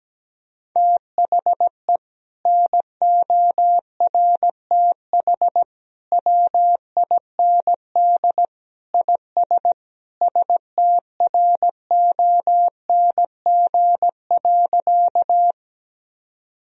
Morse code, heard as a keyed tone, about 17 wpm, 700 Hz. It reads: THE NORTH WIND IS STRONG.
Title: Sentences from Top 1000 Words - Review Part 2 - 17wpm